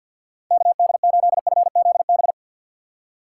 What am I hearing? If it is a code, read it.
Morse code, heard as a keyed tone, about 46 wpm, 700 Hz. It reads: KD8FZB
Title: USA Call Signs - 6 Characters - 6x Character Spacing - 50wpm